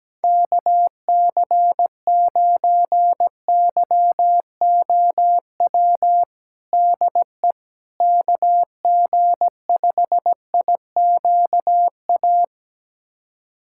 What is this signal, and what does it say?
Morse code, heard as a keyed tone, about 17 wpm, 700 Hz. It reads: KC9YOW DE KG5IQA
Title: USA Callsign DE Callsign - 6 Characters - 17wpm